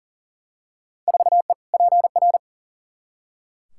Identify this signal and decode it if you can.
Morse code, heard as a keyed tone, about 40 wpm, 700 Hz. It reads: 4E PR
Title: Field Day Exchanges Encore - 40wpm